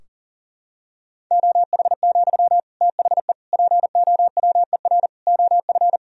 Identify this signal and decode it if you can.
Morse code, heard as a keyed tone, about 40 wpm, 700 Hz. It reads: OH, THE POWER OF
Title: Sentences from Top 1000 Words - Review Part 3 - 40wpm